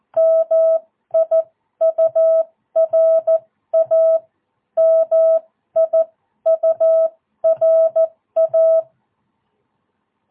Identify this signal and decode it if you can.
Morse code, heard as a keyed tone, about 14 wpm, spaced as for 12 wpm, 650 Hz. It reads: MIURA MIURA
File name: jh8cbh / 2022cw